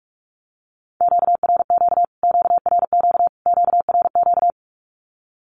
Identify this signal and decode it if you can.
Morse code, heard as a keyed tone, about 44 wpm, 700 Hz. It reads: QRQ QRQ QRQ
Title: Q-Codes Repeated 3x 45wpm